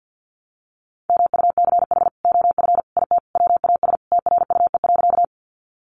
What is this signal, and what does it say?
Morse code, heard as a keyed tone, about 50 wpm, 700 Hz. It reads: MUCH OF IT WAS TRUE.
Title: Sentences from Top 400 Words 50wpm